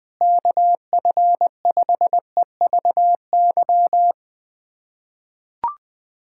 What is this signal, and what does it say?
Morse code, heard as a keyed tone, about 21 wpm, 700 Hz. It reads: KF5EVY
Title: USA Callsign DE Callsign 20wpm